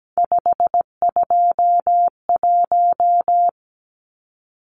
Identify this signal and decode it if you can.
Morse code, heard as a keyed tone, about 17 wpm, 700 Hz. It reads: 521
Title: Numbers - 3-Digit 17wpm